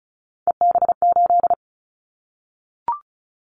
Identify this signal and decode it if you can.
Morse code, heard as a keyed tone, about 34 wpm, 700 Hz. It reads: EB8
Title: Callsign Prefixes for Top 100 DXCC - 35wpm